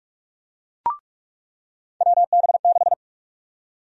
Morse code, heard as a keyed tone, about 46 wpm, 700 Hz. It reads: WB6
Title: USA Call Sign Prefixes - 45wpm